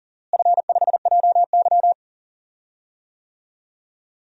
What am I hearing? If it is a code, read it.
Morse code, heard as a keyed tone, about 37 wpm, 700 Hz. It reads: F5JY
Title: DXCC-100 Call Signs - 4 Characters - Repeated 2x - 40wpm